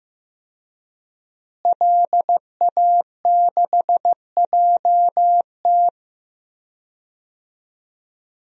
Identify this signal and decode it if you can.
Morse code, heard as a keyed tone, about 15 wpm, 700 Hz. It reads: LA6JT
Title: DXCC-100 Call Signs - 5 Characters - Repeated 3x - 15wpm